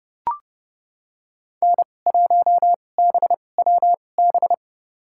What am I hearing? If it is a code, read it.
Morse code, heard as a keyed tone, about 30 wpm, spaced as for 22 wpm, 700 Hz. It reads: N1BWB
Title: USA Call Signs - 5 Characters - 2x Character Spacing - 30wpm